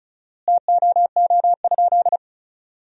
Morse code, heard as a keyed tone, about 36 wpm, 700 Hz. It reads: TOO?
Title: 4x Word-Spacing - Sentences from Top 200 Words - 35wpm